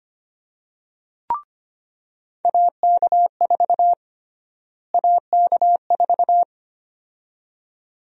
Morse code, heard as a keyed tone, about 25 wpm, 700 Hz. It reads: AK4 AK4
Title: USA Call Sign Prefixes - Repeated 2x - 25wpm